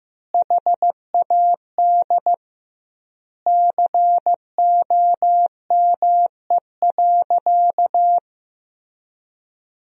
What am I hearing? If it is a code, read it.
Morse code, heard as a keyed tone, about 16 wpm, 700 Hz. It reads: HAD COME.
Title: 2x Word-Spacing - Sentences from Top 100 Words - 15wpm